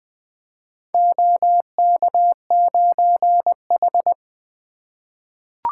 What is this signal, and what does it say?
Morse code, heard as a keyed tone, about 20 wpm, 700 Hz. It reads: OK9H
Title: Four Letter-Number Combinations - 20wpm